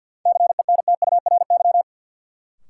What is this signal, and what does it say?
Morse code, heard as a keyed tone, about 48 wpm, 700 Hz. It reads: CENTURY
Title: Sets of 5 Words - New Word Drill - 50wpm